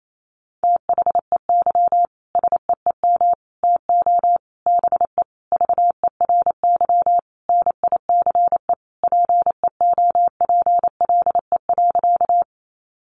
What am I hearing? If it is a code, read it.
Morse code, heard as a keyed tone, about 28 wpm, 700 Hz. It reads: THEY SEEM TO BE VERY NICE PEOPLE.
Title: Sentences from Top 1100 Words 28wpm